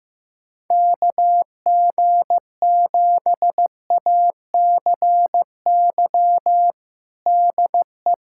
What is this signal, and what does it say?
Morse code, heard as a keyed tone, about 15 wpm, 700 Hz. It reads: KG7ACY DE
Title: USA Callsign DE Callsign - 6 Characters - 15wpm